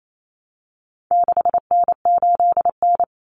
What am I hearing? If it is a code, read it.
Morse code, heard as a keyed tone, about 28 wpm, 700 Hz. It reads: 6N8N
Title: Common Cut Numbers - 4-Digit 28wpm